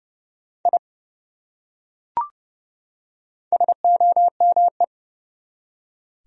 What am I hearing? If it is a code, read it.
Morse code, heard as a keyed tone, about 30 wpm, 700 Hz. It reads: I SOME
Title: Top 100 Words 30wpm